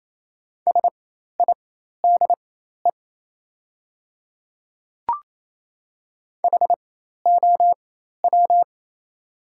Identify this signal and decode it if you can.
Morse code, heard as a keyed tone, about 27 wpm, spaced as for 13 wpm, 700 Hz. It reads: SIDE HOW
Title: 4x Character Spacing - Top 100 Words - 28wpm